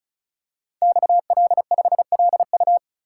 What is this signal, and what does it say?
Morse code, heard as a keyed tone, about 36 wpm, 690 Hz. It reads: XL5LU
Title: Code Groups - 5 Letter:Numbers - 35wpm